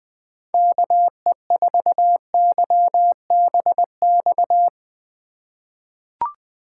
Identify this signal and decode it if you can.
Morse code, heard as a keyed tone, about 20 wpm, 700 Hz. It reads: KE4YBX
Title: USA Call Signs - 6 Characters - 2x Character Spacing - 20wpm